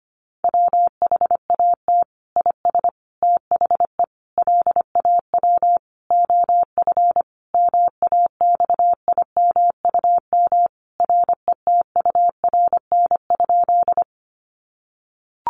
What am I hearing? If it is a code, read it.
Morse code, heard as a keyed tone, about 25 wpm, 700 Hz. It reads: WHAT IS THE LAW OF MAXIMUM RETURN?
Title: Sentences from Top 1300 Words 25wpm